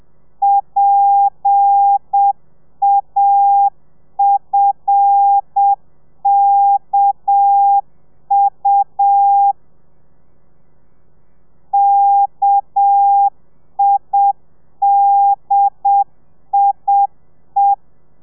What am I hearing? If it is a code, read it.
Morse code, heard as a keyed tone, about 7 wpm, 800 Hz. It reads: PAFKU KIDIE
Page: Lezione 1: audio a 30 CPM